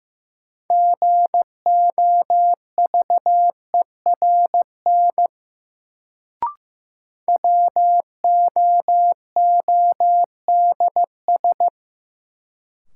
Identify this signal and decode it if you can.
Morse code, heard as a keyed tone, about 16 wpm, 700 Hz. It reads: GOVERN WOODS
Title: Sentences from Top 1000 Words - Review Part 2 - New Word Drill - 15wpm